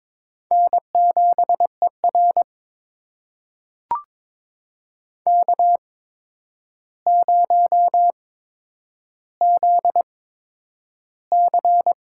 Morse code, heard as a keyed tone, about 22 wpm, spaced as for 9 wpm, 700 Hz. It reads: N7ER K0ZC
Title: USA Call Signs - 4 Characters - 8x Character Spacing - 22wpm